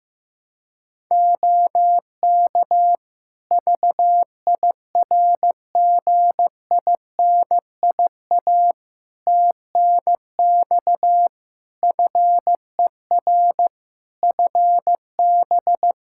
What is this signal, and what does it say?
Morse code, heard as a keyed tone, about 15 wpm, 700 Hz. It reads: OK VIRGINIA TNX FER FB
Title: Practice QSO - Exchange 4 - Ending - 15wpm